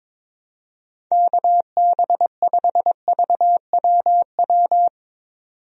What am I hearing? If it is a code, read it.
Morse code, heard as a keyed tone, about 21 wpm, 700 Hz. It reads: KB5VWW
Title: USA Call Signs - 6 Characters - 22wpm